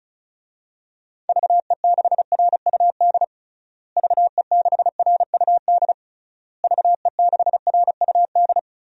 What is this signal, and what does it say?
Morse code, heard as a keyed tone, about 36 wpm, 700 Hz. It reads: VE6RUD VE6RUD VE6RUD
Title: DXCC-100 Call Signs - 6 Characters - Repeated 3x - 35wpm